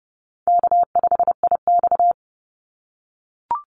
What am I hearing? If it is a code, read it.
Morse code, heard as a keyed tone, about 30 wpm, 700 Hz. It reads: K5IX
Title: Call Signs - 30wpm